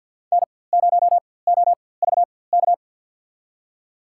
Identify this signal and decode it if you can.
Morse code, heard as a keyed tone, about 52 wpm, spaced as for 24 wpm, 700 Hz. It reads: N0OVK
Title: USA Call Signs - 4x Character Spacing - 50wpm